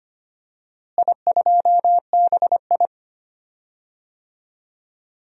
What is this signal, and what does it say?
Morse code, heard as a keyed tone, about 25 wpm, 700 Hz. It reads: I2BI
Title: DXCC-100 Call Signs - 4 Characters - Repeated 2x - 25wpm